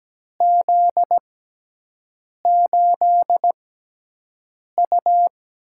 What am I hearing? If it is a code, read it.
Morse code, heard as a keyed tone, about 17 wpm, spaced as for 6 wpm, 700 Hz. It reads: Z8U
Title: USA Call Signs - 4 Characters - 6x Character Spacing - 17wpm